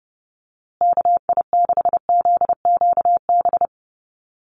Morse code, heard as a keyed tone, about 30 wpm, 700 Hz. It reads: KI6ZQB
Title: USA Call Signs Repeated 2x 30wpm